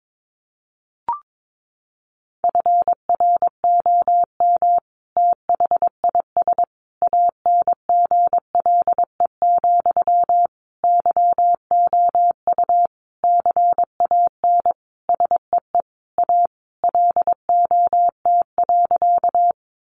Morse code, heard as a keyed tone, about 22 wpm, 700 Hz. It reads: FROM THIS ANGLE, YOU CAN SEE A LOT.
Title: Sentences from Top 1500 Words 22wpm